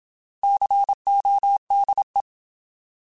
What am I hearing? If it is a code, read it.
Morse code, heard as a keyed tone, about 27 wpm, 800 Hz. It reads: CODE
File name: Alphabetical